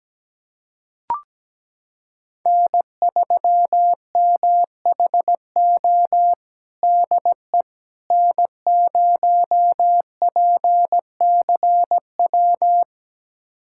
Morse code, heard as a keyed tone, about 17 wpm, 700 Hz. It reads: N3MHO DE N0PCW
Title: USA Callsign DE Callsign - 5 Characters - 17wpm